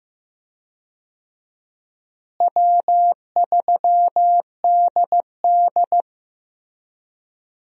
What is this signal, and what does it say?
Morse code, heard as a keyed tone, about 15 wpm, 700 Hz. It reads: W3DD
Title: USA Call Signs - 4 Characters - Repeated 2x - 15wpm